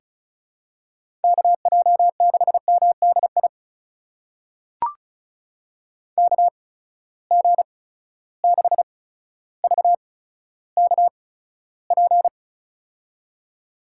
Morse code, heard as a keyed tone, about 36 wpm, spaced as for 15 wpm, 700 Hz. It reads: KJ6MDI KG6VKP
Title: USA Call Signs - 6 Characters - 8x Character Spacing - 35wpm